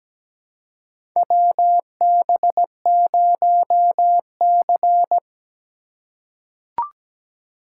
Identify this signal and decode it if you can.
Morse code, heard as a keyed tone, about 17 wpm, 700 Hz. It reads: WB0C
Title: USA Call Signs - 4 Characters - 2x Character Spacing - 17wpm